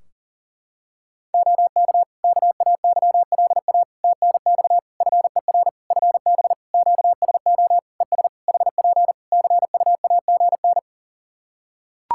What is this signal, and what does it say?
Morse code, heard as a keyed tone, about 37 wpm, 700 Hz. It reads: OK KAYLA TNX FER FB QSO ES HP CUAGN
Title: Practice QSO - Exchange 4 - Ending - 40wpm